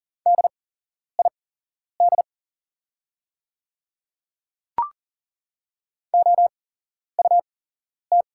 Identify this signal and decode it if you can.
Morse code, heard as a keyed tone, about 40 wpm, spaced as for 11 wpm, 700 Hz. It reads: DID OUT
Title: 8x Character Spacing - Top 100 Words - 40wpm